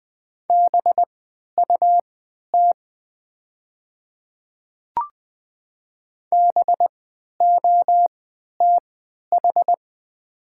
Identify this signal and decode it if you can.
Morse code, heard as a keyed tone, about 20 wpm, spaced as for 11 wpm, 700 Hz. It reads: BUT BOTH
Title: 3x Character Spacing - Top 300 Words - 20wpm